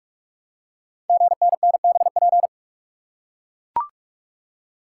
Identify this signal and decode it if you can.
Morse code, heard as a keyed tone, about 44 wpm, 690 Hz. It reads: GNNBP
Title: Code Groups - 5 Letters - 45wpm